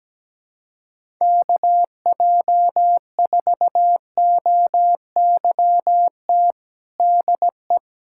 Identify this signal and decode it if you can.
Morse code, heard as a keyed tone, about 17 wpm, 700 Hz. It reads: KJ4OYT DE
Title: USA Callsign DE Callsign - 6 Characters - 17wpm